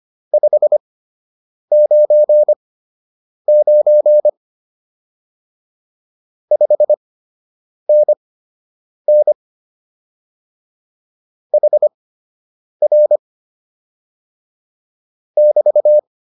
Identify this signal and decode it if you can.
Morse code, heard as a keyed tone, about 25 wpm, spaced as for 8 wpm, 600 Hz. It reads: 599 5NN HR <BT>